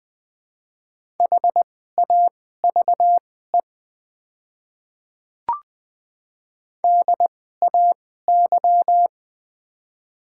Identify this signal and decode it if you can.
Morse code, heard as a keyed tone, about 20 wpm, spaced as for 14 wpm, 700 Hz. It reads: HAVE DAY
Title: 2x Character Spacing - Top 100 Words - 20wpm